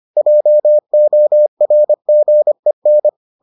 Morse code, heard as a keyed tone, about 25 wpm, 600 Hz. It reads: JORGEN